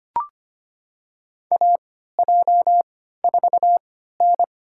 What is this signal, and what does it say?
Morse code, heard as a keyed tone, about 25 wpm, spaced as for 14 wpm, 700 Hz. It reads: AJ4N
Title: USA Call Signs - 4 Characters - 3x Character Spacing - 25wpm